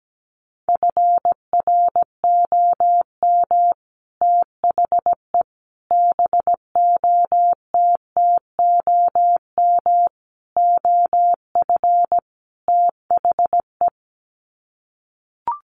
Morse code, heard as a keyed tone, about 17 wpm, 700 Hz. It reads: FROM THE BOTTOM OF THE
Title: Sets of 5 Words Encore2 17wpm